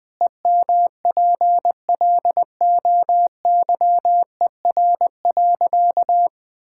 Morse code, heard as a keyed tone, about 20 wpm, 700 Hz. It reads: EMPLOYER.
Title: Pangram Challenge v2 - 20wpm